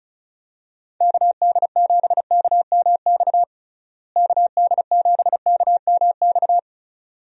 Morse code, heard as a keyed tone, about 36 wpm, 700 Hz. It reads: KD7KMX KD7KMX
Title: USA Call Signs - 6 Characters - Repeated 2x - 35wpm